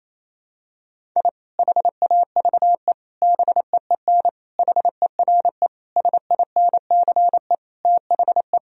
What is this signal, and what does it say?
Morse code, heard as a keyed tone, about 28 wpm, 700 Hz. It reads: I HAVE BEEN HERE SINCE THE
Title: Sentences from Top 1000 Words - Review Part 1 - 28wpm